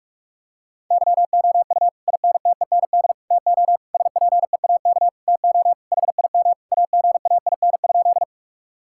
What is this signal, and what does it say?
Morse code, heard as a keyed tone, about 41 wpm, 700 Hz. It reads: YOU INTEND TO SPEAK TO HIM AGAIN?
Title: Sentences from Top 1600 Words 45wpm